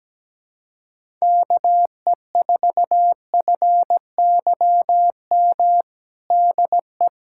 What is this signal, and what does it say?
Morse code, heard as a keyed tone, about 17 wpm, 700 Hz. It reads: KE4FYM DE
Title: USA Callsign DE Callsign - 6 Characters - 17wpm